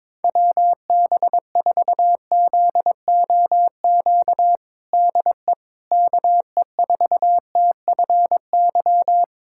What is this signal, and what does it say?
Morse code, heard as a keyed tone, about 22 wpm, 700 Hz. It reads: WB4ZOQ DE KE4TFY
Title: USA Callsign DE Callsign 22wpm